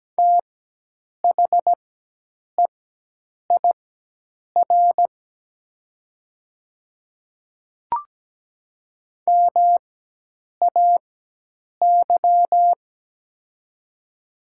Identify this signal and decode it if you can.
Morse code, heard as a keyed tone, about 17 wpm, spaced as for 8 wpm, 700 Hz. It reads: THEIR MAY
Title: 4x Character Spacing - Top 100 Words - 17wpm